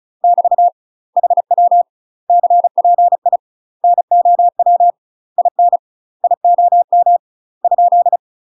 Morse code, heard as a keyed tone, about 36 wpm, 700 Hz. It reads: <BT> HW CPI NOW IN IOM ?
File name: UK MASTER QSOs TRAINING / 35WPM / UK MASTER QSOs 0011